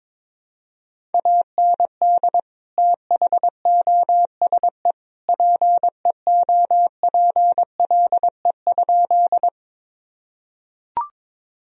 Morse code, heard as a keyed tone, about 22 wpm, 700 Hz. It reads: AND THOSE PEOPLE?
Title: Sentences from Top 1000 Words - Review Part 1 - 22wpm